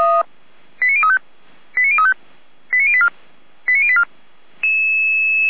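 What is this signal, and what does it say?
Five-tone selective call (ZVEI), heard as quick runs of five tones, tones of about 70 ms. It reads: 89035 89035 89084 89084
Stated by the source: ZVEI five-tone call 89092 received by SDR.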